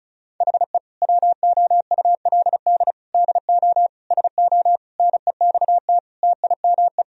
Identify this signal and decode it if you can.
Morse code, heard as a keyed tone, about 36 wpm, 700 Hz. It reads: HE WOULD DO SO NEXT TIME
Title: Sentences from Top 1000 Words - Review Part 1 - 35wpm